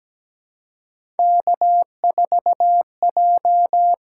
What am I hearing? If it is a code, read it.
Morse code, heard as a keyed tone, about 17 wpm, 700 Hz. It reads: K4J
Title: USA Call Signs - 5 Characters - 6x Character Spacing - 17wpm